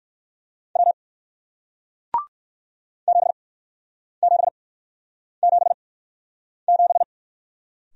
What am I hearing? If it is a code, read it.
Morse code, heard as a keyed tone, about 57 wpm, spaced as for 9 wpm, 700 Hz. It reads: W 7777